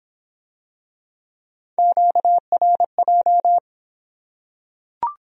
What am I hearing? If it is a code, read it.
Morse code, heard as a keyed tone, about 26 wpm, 700 Hz. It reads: QRJ